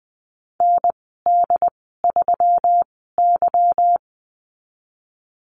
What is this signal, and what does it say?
Morse code, heard as a keyed tone, about 20 wpm, spaced as for 15 wpm, 700 Hz. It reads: ND3Y
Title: 173 Lesson 28g - Number 9 - Call Signs 20wpm